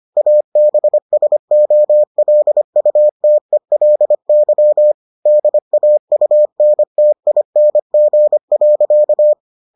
Morse code, heard as a keyed tone, about 25 wpm, 600 Hz. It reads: ABSOLUTELY DAUNTING.